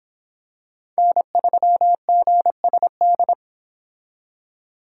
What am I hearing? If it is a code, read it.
Morse code, heard as a keyed tone, about 26 wpm, 700 Hz. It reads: N3GSD